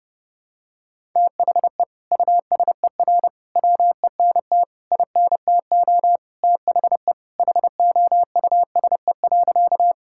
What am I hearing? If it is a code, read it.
Morse code, heard as a keyed tone, about 29 wpm, 700 Hz. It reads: THE USER WENT INTO THE HOUSE.
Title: Sentences from Top 1000 Words - Review Part 5 - 30wpm